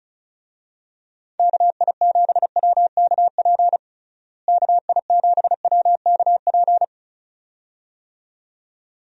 Morse code, heard as a keyed tone, about 34 wpm, 700 Hz. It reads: KI7WKP KI7WKP
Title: USA Call Signs - 6 Characters - Repeated 2x - 35wpm